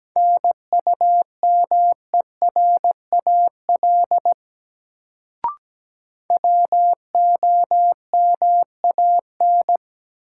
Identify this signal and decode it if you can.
Morse code, heard as a keyed tone, about 17 wpm, 700 Hz. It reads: NUMERAL WOMAN
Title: Sentences from Top 1000 Words - Review Part 2 - New Word Drill - 17wpm